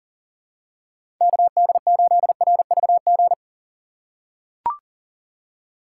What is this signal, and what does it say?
Morse code, heard as a keyed tone, about 40 wpm, 700 Hz. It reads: KD8RVG
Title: USA Call Signs - 6 Characters - 2x Character Spacing - 40wpm